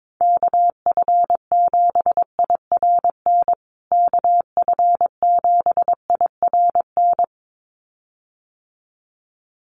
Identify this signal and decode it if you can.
Morse code, heard as a keyed tone, about 22 wpm, 700 Hz. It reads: KF7IRN KF7IRN
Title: USA Call Signs Repeated 2x 22wpm